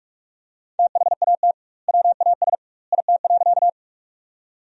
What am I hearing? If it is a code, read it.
Morse code, heard as a keyed tone, about 46 wpm, 700 Hz. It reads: THAT WAS IT.
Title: 2x Word-Spacing - Sentences from Top 100 Words - 45wpm